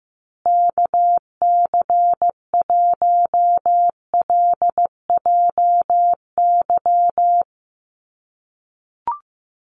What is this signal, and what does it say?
Morse code, heard as a keyed tone, about 14 wpm, 700 Hz. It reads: KC1LJY
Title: USA Call Signs Repeated 3x 15wpm